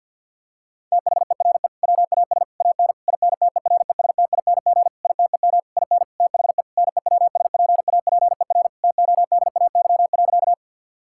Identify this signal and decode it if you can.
Morse code, heard as a keyed tone, about 50 wpm, 700 Hz. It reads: THERE WAS AN INTERESTING ITEM IN THE NEWSPAPER TODAY.
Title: Sentences from Top 1000 Words - Review Part 6 - 50wpm